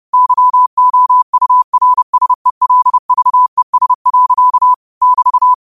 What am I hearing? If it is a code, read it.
Morse code, heard as a keyed tone, about 30 wpm, 1000 Hz. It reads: YOURSELVES. <BT>
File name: qotdfiles / 30wpm